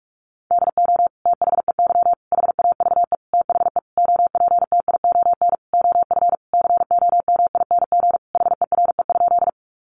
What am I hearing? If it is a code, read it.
Morse code, heard as a keyed tone, about 44 wpm, 700 Hz. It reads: DO THEY HAVE THE OPTION OF COMING HERE?
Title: Sentences from Top 1300 Words 45wpm